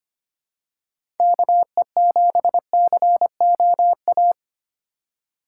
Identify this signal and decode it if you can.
Morse code, heard as a keyed tone, about 25 wpm, 700 Hz. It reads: KE7COA
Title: USA Call Signs - 4x Character Spacing - 25wpm